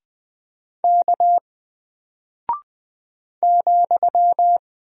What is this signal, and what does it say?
Morse code, heard as a keyed tone, about 20 wpm, 700 Hz. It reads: K ,
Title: Single Character (Letter:Number:Punctuation:Prosign) 20wpm